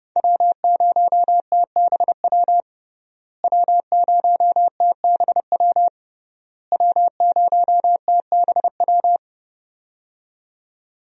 Morse code, heard as a keyed tone, about 32 wpm, 700 Hz. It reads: W0TBW W0TBW W0TBW
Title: USA Call Signs - 5 Characters - Repeated 3x - 30wpm